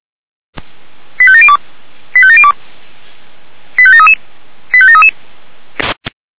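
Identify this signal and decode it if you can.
Five-tone selective call (ZVEI), heard as quick runs of five tones, tones of about 70 ms. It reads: 85702 85702 85733 85733
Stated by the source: ZVEI five-tone selective call received on SDR.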